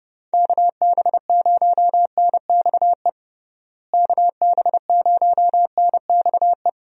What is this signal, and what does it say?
Morse code, heard as a keyed tone, about 30 wpm, 700 Hz. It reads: KB0NXE KB0NXE
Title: USA Call Signs - 6 Characters - Repeated 3x - 30wpm